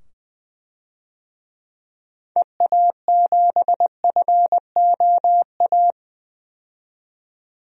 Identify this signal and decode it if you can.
Morse code, heard as a keyed tone, about 20 wpm, 700 Hz. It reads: EA7FOA
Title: DXCC-100 Call Signs - 6 Characters - 20wpm